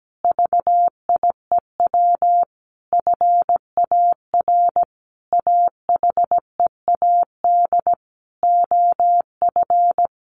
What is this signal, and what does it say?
Morse code, heard as a keyed tone, about 17 wpm, 700 Hz. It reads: VIEW FAR AHEAD OF
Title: Sentences from Top 1100 Words 17wpm